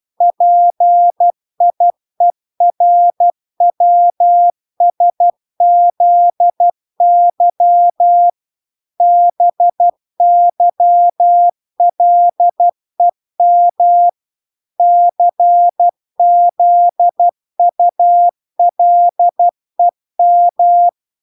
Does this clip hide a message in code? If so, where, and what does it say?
Yes: Morse code from 0.2 to 20.9 s, heard as a keyed tone, about 12 wpm, 700 Hz. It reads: PIERWSZY BYLEM CZULEM